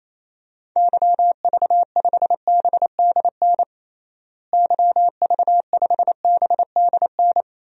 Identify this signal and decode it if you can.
Morse code, heard as a keyed tone, about 28 wpm, 700 Hz. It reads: YV5BDN YV5BDN
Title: DXCC-100 Call Signs - 6 Characters - Repeated 2x - 28wpm